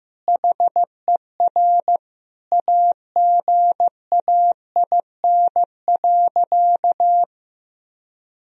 Morse code, heard as a keyed tone, about 14 wpm, 700 Hz. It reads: HER AGAIN.
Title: Sentences from Top 1000 Words - Review Part 4 - 15wpm